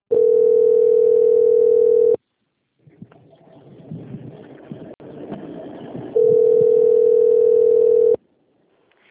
Ringback tone (North America), 440 + 480 Hz.